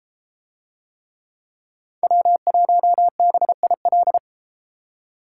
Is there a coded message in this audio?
Yes: Morse code from 2.0 to 4.2 s, heard as a keyed tone, about 29 wpm, 700 Hz. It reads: W1BIL